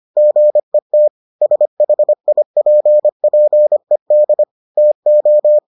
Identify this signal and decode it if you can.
Morse code, heard as a keyed tone, about 25 wpm, 600 Hz. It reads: GET SHIPPED TO